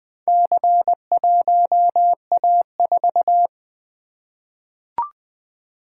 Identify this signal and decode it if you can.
Morse code, heard as a keyed tone, about 20 wpm, 700 Hz. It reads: C1A4
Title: Four Letter-Number Combinations - 20wpm